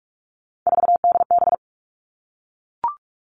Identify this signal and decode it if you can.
Morse code, heard as a keyed tone, about 44 wpm, 700 Hz. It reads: 4DB
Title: All Cut Numbers - 3-Digit 45wpm